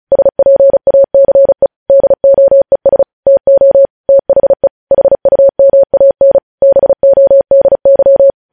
Morse code, heard as a keyed tone, about 36 wpm, 550 Hz. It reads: SPACE DOES TO THE HUMAN BODY
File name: podcast / News headlines-1x-35x35